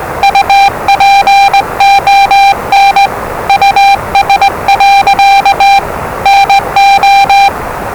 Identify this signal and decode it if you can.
Morse code, heard as a keyed tone, about 20 wpm, 790 Hz, with noise. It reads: UPON US. NO
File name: WellingtonRadioZlwCloseDownOnHfAndMf / 01-Zlw 4277khz 417.5 qrt until 0925z